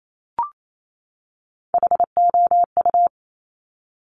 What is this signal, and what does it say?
Morse code, heard as a keyed tone, about 28 wpm, 700 Hz. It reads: HOU
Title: Top Three Letter Combinations in Words 28wpm